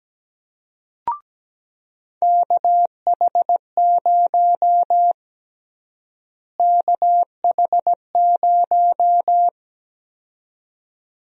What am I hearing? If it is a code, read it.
Morse code, heard as a keyed tone, about 17 wpm, 700 Hz. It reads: KH0 KH0
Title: USA Call Sign Prefixes - Repeated 2x - 17wpm